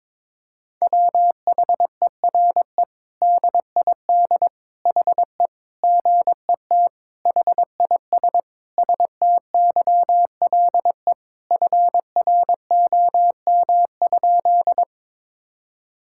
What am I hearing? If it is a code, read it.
Morse code, heard as a keyed tone, about 22 wpm, 700 Hz. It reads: WHERE DID HE GET HIS STYLE FROM?
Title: Sentences from Top 1000 Words - Review Part 5 - 22wpm